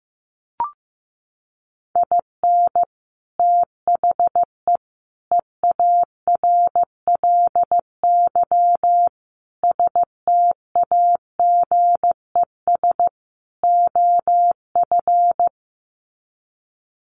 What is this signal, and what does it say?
Morse code, heard as a keyed tone, about 15 wpm, 700 Hz. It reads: IN THE EARLY STAGES OF
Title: Sets of 5 Words Encore 15wpm